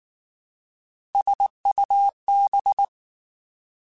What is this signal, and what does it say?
Morse code, heard as a keyed tone, about 19 wpm, 800 Hz. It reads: SUB